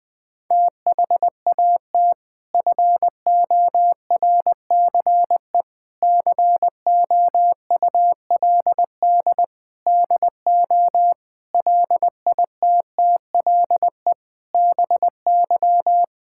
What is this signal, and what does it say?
Morse code, heard as a keyed tone, about 19 wpm, 700 Hz. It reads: THAT FORCE COULD DO LITTLE BY
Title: Sentences from Top 1000 Words - Review Part 5 - 20wpm